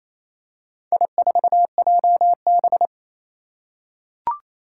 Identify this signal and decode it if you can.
Morse code, heard as a keyed tone, about 27 wpm, 700 Hz. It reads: I4JB
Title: Four Challenging Letter-Number Combinations 28wpm